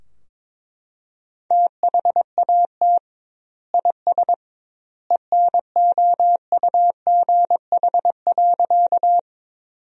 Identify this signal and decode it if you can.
Morse code, heard as a keyed tone, about 22 wpm, 700 Hz. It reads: THAT IS ENOUGH.
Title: 2x Word-Spacing - Sentences from Top 400 Words - 22wpm